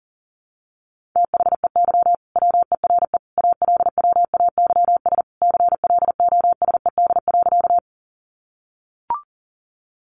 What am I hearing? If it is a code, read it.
Morse code, heard as a keyed tone, about 40 wpm, 700 Hz. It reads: THEY WERE ALWAYS CLOSED.
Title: Sentences from Top 1400 Words 40wpm